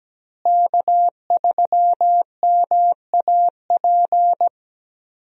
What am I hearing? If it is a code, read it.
Morse code, heard as a keyed tone, about 17 wpm, 700 Hz. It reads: K3MAP
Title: USA Call Signs - 5 Characters - Repeated 2x - 17wpm